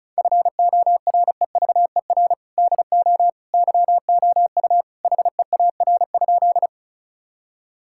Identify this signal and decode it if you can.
Morse code, heard as a keyed tone, about 36 wpm, 700 Hz. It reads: FOREVER DO YOU HEAR?
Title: Sentences from Top 1600 Words 35wpm